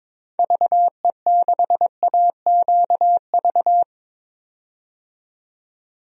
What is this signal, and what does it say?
Morse code, heard as a keyed tone, about 22 wpm, 700 Hz. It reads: VE6AQV
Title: DXCC-100 Call Signs - 6 Characters - 22wpm